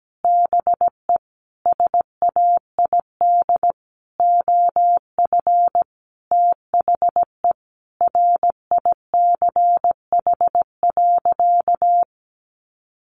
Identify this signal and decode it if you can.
Morse code, heard as a keyed tone, about 17 wpm, 700 Hz. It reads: BE SAID OF THE RICH.
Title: Sentences from Top 1100 Words 17wpm